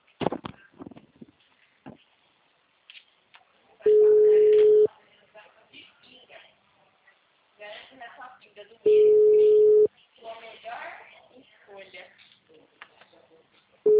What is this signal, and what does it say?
Ringback tone (Europe, 425 Hz), with speech.
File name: out- -2002- - - .108